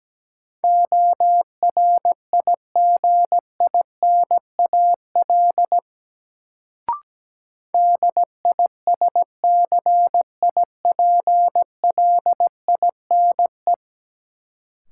Morse code, heard as a keyed tone, about 17 wpm, 700 Hz. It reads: ORIGINAL DISCIPLINE
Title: Sentences from Top 900 Words - New Word Drill - 17wpm